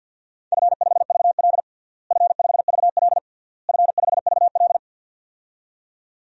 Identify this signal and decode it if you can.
Morse code, heard as a keyed tone, about 52 wpm, 700 Hz. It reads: F5VL F5VL F5VL
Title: DXCC-100 Call Signs - 4 Characters - Repeated 3x - 50wpm